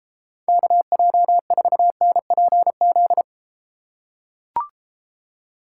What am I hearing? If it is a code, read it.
Morse code, heard as a keyed tone, about 33 wpm, 700 Hz. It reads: KJ4NPZ